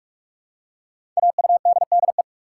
Morse code, heard as a keyed tone, about 44 wpm, 700 Hz. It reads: AUDDE
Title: Code Groups - 5 Letters - 45wpm